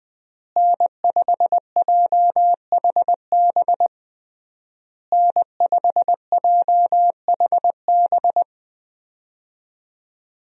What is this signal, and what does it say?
Morse code, heard as a keyed tone, about 20 wpm, 700 Hz. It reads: N5JHB N5JHB
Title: USA Call Signs - 5 Characters - Repeated 3x - 20wpm